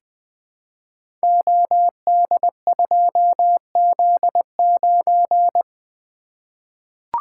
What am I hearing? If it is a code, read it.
Morse code, heard as a keyed tone, about 20 wpm, 700 Hz. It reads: OD2Z9
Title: Code Groups - 5 Letter:Numbers - 20wpm